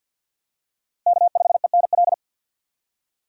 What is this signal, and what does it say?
Morse code, heard as a keyed tone, about 52 wpm, 700 Hz. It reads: K5ENL
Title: USA Call Signs - 5 Characters - Repeated 3x - 50wpm